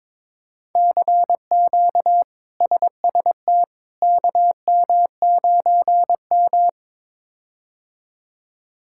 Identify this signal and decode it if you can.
Morse code, heard as a keyed tone, about 21 wpm, 700 Hz. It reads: CQ SST KM9M
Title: K1USN SST QSO Practice - 22wpm